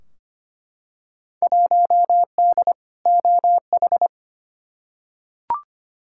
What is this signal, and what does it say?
Morse code, heard as a keyed tone, about 25 wpm, 700 Hz. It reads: 1D OH
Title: Field Day Exchanges - 25wpm